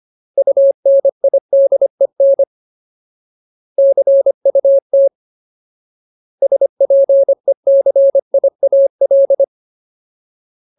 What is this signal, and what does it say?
Morse code, heard as a keyed tone, about 25 wpm, 550 Hz. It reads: UNIDEN CUT SPECIAL
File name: Ultimate Word List / 25WPM Ultimate